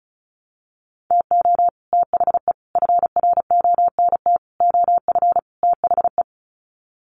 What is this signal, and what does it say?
Morse code, heard as a keyed tone, about 34 wpm, 700 Hz. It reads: TO THE FRONT OF THE
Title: Sets of 5 Words 35wpm